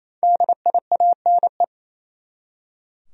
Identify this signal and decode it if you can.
Morse code, heard as a keyed tone, about 27 wpm, 700 Hz. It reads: DIANE
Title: USA Common Names - 28wpm